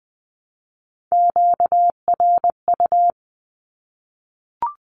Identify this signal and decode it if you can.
Morse code, heard as a keyed tone, about 20 wpm, 700 Hz. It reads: QRU
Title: Q-Codes Repeated 3x 20wpm